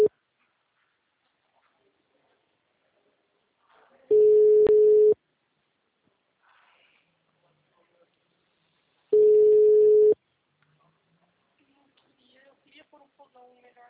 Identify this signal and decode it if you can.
Ringback tone (Europe, 425 Hz), with speech.